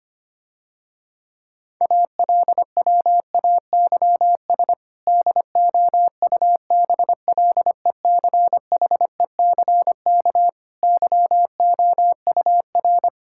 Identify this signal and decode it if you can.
Morse code, heard as a keyed tone, about 25 wpm, 700 Hz. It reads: ALWAYS DOUBLECHECK YOUR
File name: practice / 25wpm / Difficult Sentences with Hints - 25wpm